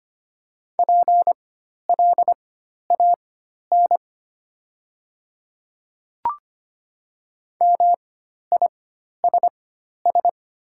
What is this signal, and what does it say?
Morse code, heard as a keyed tone, about 25 wpm, spaced as for 12 wpm, 700 Hz. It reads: PLAN MISS
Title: 4x Character Spacing - Top 500 Words - 25wpm